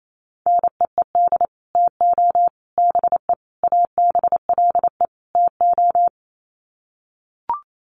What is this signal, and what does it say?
Morse code, heard as a keyed tone, about 28 wpm, 700 Hz. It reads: NEED TO BE ABLE TO
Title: Sets of 5 Words Encore 28wpm